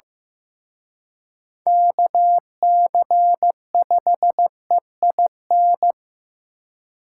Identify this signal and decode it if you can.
Morse code, heard as a keyed tone, about 18 wpm, 700 Hz. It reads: KC5EIN
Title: Speed Racing - USA 2x3 Call Signs 15wpm